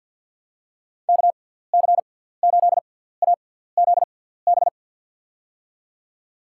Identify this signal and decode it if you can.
Morse code, heard as a keyed tone, about 47 wpm, spaced as for 17 wpm, 700 Hz. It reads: KC8AZB
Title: USA Call Signs - 6 Characters - 6x Character Spacing - 50wpm